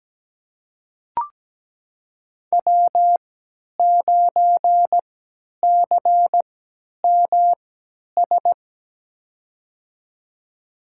Morse code, heard as a keyed tone, about 17 wpm, spaced as for 10 wpm, 700 Hz. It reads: W9CMS
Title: USA Call Signs - 5 Characters - 3x Character Spacing - 17wpm